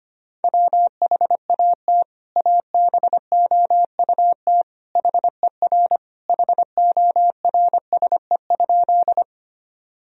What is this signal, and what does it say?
Morse code, heard as a keyed tone, about 25 wpm, 700 Hz. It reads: WHAT ABOUT HER HORSE?
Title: Sentences from Top 1000 Words - Review Part 1 - 25wpm